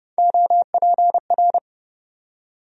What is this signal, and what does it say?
Morse code, heard as a keyed tone, about 30 wpm, 700 Hz. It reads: OPR
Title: USA Call Sign Postfixes - 3 Letters 30wpm